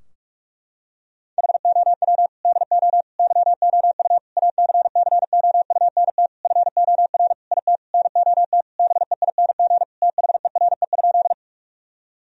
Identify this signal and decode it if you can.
Morse code, heard as a keyed tone, about 47 wpm, 700 Hz. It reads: HOW DO YOU ACCOUNT FOR IT NOT BEING THERE?
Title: Sentences from Top 1000 Words - Review Part 3 - 45wpm